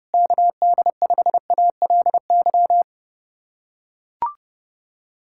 Morse code, heard as a keyed tone, about 32 wpm, 700 Hz. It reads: KD5ALY
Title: USA Callsign DE Callsign 30wpm